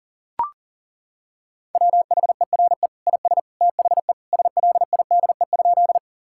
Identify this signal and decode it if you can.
Morse code, heard as a keyed tone, about 40 wpm, 700 Hz. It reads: WHERE IS THE SLIDE?
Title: Sentences from Top 1600 Words 40wpm